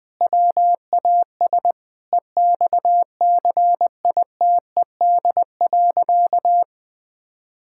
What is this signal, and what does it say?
Morse code, heard as a keyed tone, about 20 wpm, 700 Hz. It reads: WAS EXCITED.
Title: Sentences with 5-Word Phrases - Spoken Beginning - 20wpm